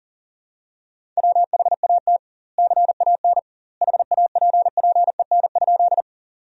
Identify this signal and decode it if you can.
Morse code, heard as a keyed tone, about 40 wpm, 700 Hz. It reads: WHAT CAN HAPPEN?
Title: 2x Word-Spacing - Sentences from Top 400 Words - 40wpm